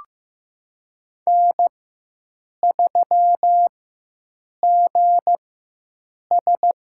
Morse code, heard as a keyed tone, about 15 wpm, spaced as for 7 wpm, 700 Hz. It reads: N3GS
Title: USA Call Signs - 4x Character Spacing - 15wpm